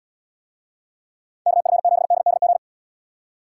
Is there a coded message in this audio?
Yes: Morse code from 1.5 to 2.6 s, heard as a keyed tone, about 78 wpm, 700 Hz. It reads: KF7DDG